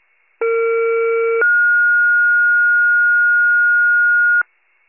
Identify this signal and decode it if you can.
Two-tone page, heard as a one-second tone, then a three-second tone, about 460 then 1430 Hz. It is heard from 0.4 to 4.4 s.